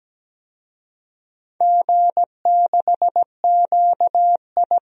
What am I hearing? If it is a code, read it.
Morse code, heard as a keyed tone, about 17 wpm, 700 Hz. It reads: G6QI
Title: DXCC-100 Call Signs - 4 Characters - 17wpm